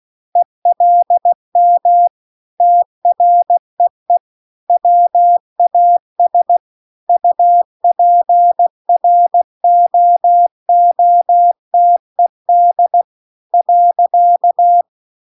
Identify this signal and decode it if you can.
Morse code, heard as a keyed tone, about 16 wpm, 700 Hz. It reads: ELM TREE WAS UPROOTED .
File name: CW-The-Big-Twoheaded-River-2-16WPM